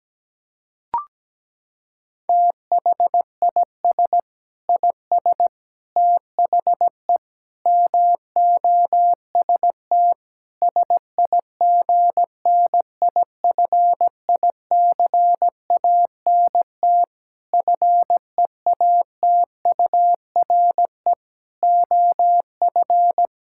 Morse code, heard as a keyed tone, about 17 wpm, 700 Hz. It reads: THIS IS THE MOST SIGNIFICANT FEATURE OF
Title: Sentences from Top 1000 Words - Review Part 6 - 17wpm